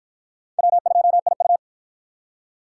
Morse code, heard as a keyed tone, about 50 wpm, 700 Hz. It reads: W2IU